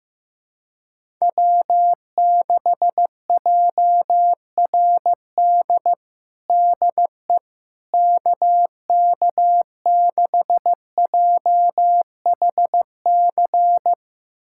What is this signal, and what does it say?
Morse code, heard as a keyed tone, about 15 wpm, 700 Hz. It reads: W6JRD DE KK6JHC
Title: USA Callsign DE Callsign 15wpm